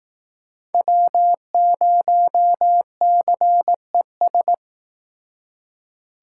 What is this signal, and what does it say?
Morse code, heard as a keyed tone, about 19 wpm, 700 Hz. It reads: W0CES